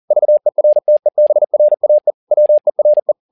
Morse code, heard as a keyed tone, about 40 wpm, 600 Hz. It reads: VERTEBRAE WERE
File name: learning center / tower trouble / tower touble 40wpm